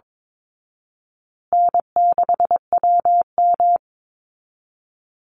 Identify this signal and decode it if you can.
Morse code, heard as a keyed tone, about 22 wpm, 700 Hz. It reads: N6WM
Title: Easy Call Signs Repeated 3x - 22wpm